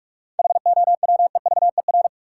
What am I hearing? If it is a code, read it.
Morse code, heard as a keyed tone, about 43 wpm, 700 Hz. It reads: HOWEVER
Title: Sentences from Top 600 Words - New Word Drill - 45wpm